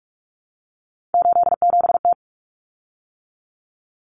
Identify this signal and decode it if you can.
Morse code, heard as a keyed tone, about 44 wpm, 700 Hz. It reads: 87T
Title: Common Cut Numbers - 3-Digit 45wpm